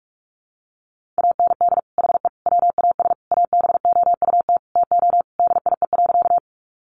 Morse code, heard as a keyed tone, about 44 wpm, 700 Hz. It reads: AND HE WAS ABOUT TO DIE.
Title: Sentences from Top 1300 Words 45wpm